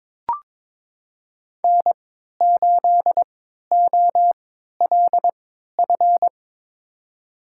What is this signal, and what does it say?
Morse code, heard as a keyed tone, about 22 wpm, spaced as for 12 wpm, 700 Hz. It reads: N8OLF
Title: USA Call Signs - 5 Characters - 3x Character Spacing - 22wpm